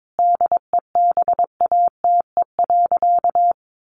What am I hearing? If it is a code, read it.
Morse code, heard as a keyed tone, about 22 wpm, 700 Hz. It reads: DEBATE.
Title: Sentences from Top 1100 Words 22wpm